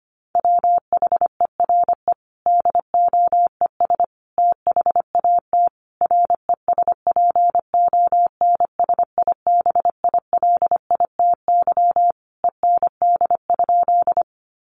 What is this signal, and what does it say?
Morse code, heard as a keyed tone, about 25 wpm, 700 Hz. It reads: WHERE DOES THAT RESPONSIBILITY END?
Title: Sentences from Top 1200 Words 25wpm